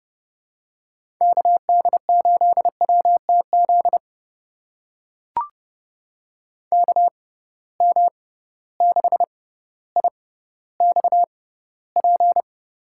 Morse code, heard as a keyed tone, about 30 wpm, spaced as for 15 wpm, 700 Hz. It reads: KD8WTZ KM6IXP
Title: USA Call Signs - 6 Characters - 6x Character Spacing - 30wpm